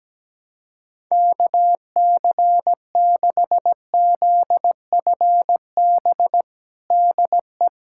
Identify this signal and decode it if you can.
Morse code, heard as a keyed tone, about 17 wpm, 700 Hz. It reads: KC6ZFB DE
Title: USA Callsign DE Callsign 17wpm